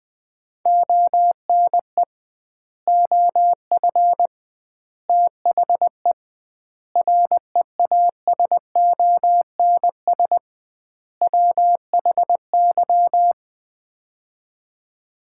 Morse code, heard as a keyed tone, about 20 wpm, 700 Hz. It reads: ONE OF THE REASONS WHY
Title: 2x Word-Spacing - Sets of 5 Words - 20wpm